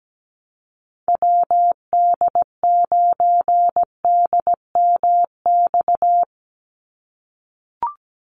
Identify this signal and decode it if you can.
Morse code, heard as a keyed tone, about 17 wpm, 700 Hz. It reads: WD9DMX
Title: USA Call Signs 17wpm